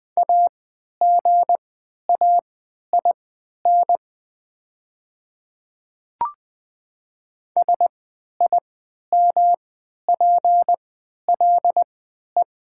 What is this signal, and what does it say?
Morse code, heard as a keyed tone, about 20 wpm, spaced as for 11 wpm, 700 Hz. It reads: AGAIN SIMPLE
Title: 3x Character Spacing - Top 500 Words - 20wpm